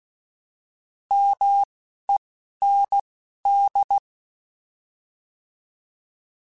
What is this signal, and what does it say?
Morse code, heard as a keyed tone, about 16 wpm, 800 Hz. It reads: M E N D
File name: Alphabetical / Alphabetical mp3s / Ms @10